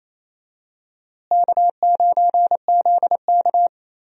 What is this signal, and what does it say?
Morse code, heard as a keyed tone, about 28 wpm, 700 Hz. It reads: K9ZK
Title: USA Call Signs - 4 Characters - Repeated 2x - 28wpm